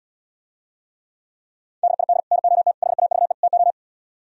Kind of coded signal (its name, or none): Morse code